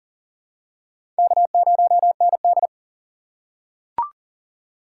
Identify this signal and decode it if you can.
Morse code, heard as a keyed tone, about 40 wpm, 700 Hz. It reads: K0ND
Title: USA Call Signs - 4 Characters - 3x Character Spacing - 40wpm